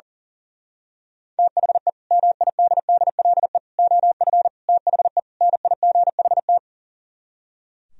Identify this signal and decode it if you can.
Morse code, heard as a keyed tone, about 38 wpm, 700 Hz. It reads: THE MIDDLE OF THE NIGHT
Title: Speed-Racing - Sets of 5 Words 40wpm